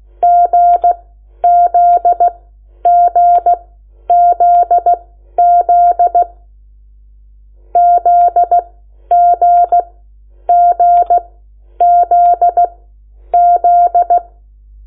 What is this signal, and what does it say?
Morse code, heard as a keyed tone, about 16 wpm, spaced as for 10 wpm, 700 Hz. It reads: GZGZZ ZGGZZ
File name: Lecciones / CLASE Nº8